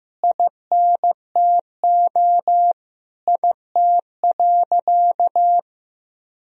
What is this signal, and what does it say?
Morse code, heard as a keyed tone, about 16 wpm, 700 Hz. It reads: INTO IT.
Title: Sentences from Top 1000 Words - Review Part 1 - 15wpm